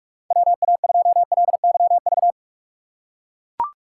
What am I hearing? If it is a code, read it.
Morse code, heard as a keyed tone, about 42 wpm, 700 Hz. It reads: WA2LYV
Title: USA Call Signs - 6 Characters - 3x Character Spacing - 45wpm